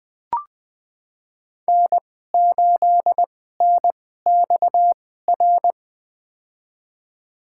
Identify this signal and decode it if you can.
Morse code, heard as a keyed tone, about 20 wpm, spaced as for 15 wpm, 700 Hz. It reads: N8NXR
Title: USA Call Signs - 5 Characters - 2x Character Spacing - 20wpm